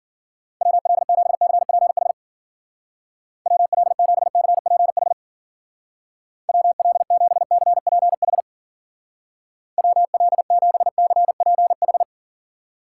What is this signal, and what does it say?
Morse code, heard as a keyed tone, about 49 wpm, 700 Hz. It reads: WL7CPH WL7CPH WL7CPH WL7CPH
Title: Speed Racing - USA 2x3 Call Signs 40wpm